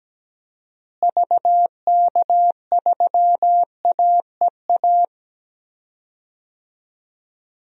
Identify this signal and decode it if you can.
Morse code, heard as a keyed tone, about 17 wpm, 700 Hz. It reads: VK3AEA
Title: DXCC-100 Call Signs - 6 Characters - Repeated 2x - 17wpm